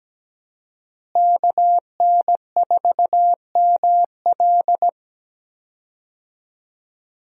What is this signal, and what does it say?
Morse code, heard as a keyed tone, about 17 wpm, 700 Hz. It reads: KN4ML
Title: USA Call Signs - 5 Characters - Repeated 2x - 17wpm